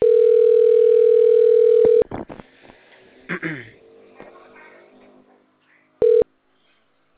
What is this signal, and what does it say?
Ringback tone (North America), 440 + 480 Hz.